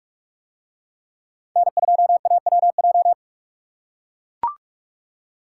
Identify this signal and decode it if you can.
Morse code, heard as a keyed tone, about 41 wpm, 700 Hz. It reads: N2AWJ